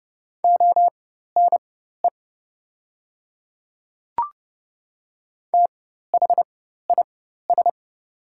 Morse code, heard as a keyed tone, about 30 wpm, spaced as for 14 wpm, 700 Hz. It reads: ONE THIS
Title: 4x Character Spacing - Top 300 Words - 30wpm